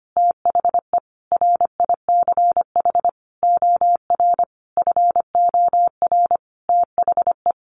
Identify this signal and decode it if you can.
Morse code, heard as a keyed tone, about 25 wpm, 700 Hz. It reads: THE RICH OR FOR THE
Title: Sentences from Top 1100 Words 25wpm